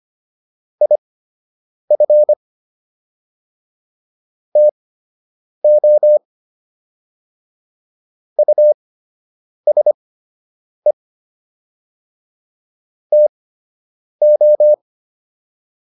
Morse code, heard as a keyed tone, about 25 wpm, spaced as for 8 wpm, 600 Hz. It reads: IF TO USE TO